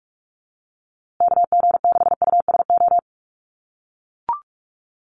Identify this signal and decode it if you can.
Morse code, heard as a keyed tone, about 44 wpm, 690 Hz. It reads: KG6USO
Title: USA Call Signs Repeated 3x 45wpm